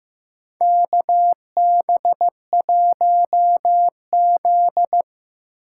Morse code, heard as a keyed tone, about 15 wpm, 700 Hz. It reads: K B 1 Z
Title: USA Call Signs - 4 Characters - 6x Character Spacing - 15wpm